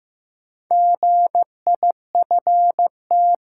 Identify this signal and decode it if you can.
Morse code, heard as a keyed tone, about 15 wpm, 700 Hz. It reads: GIFT
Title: Sentences from Top 900 Words - New Word Drill - 15wpm